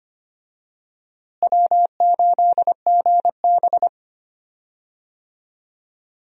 Morse code, heard as a keyed tone, about 25 wpm, 700 Hz. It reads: W8GB
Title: USA Call Signs - 4 Characters - Repeated 2x - 25wpm